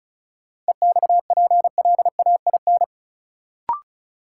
Morse code, heard as a keyed tone, about 34 wpm, 700 Hz. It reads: EXPLAIN
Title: Sentences from Top 1100 Words - New Word Drill - 35wpm